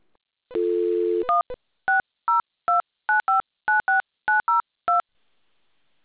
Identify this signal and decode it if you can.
DTMF, heard as telephone touch tones, tones of about 150 ms. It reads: 16029596902